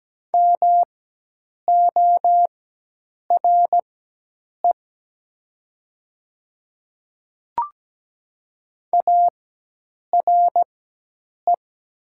Morse code, heard as a keyed tone, about 17 wpm, spaced as for 8 wpm, 700 Hz. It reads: MORE ARE
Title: 4x Character Spacing - Top 100 Words - 17wpm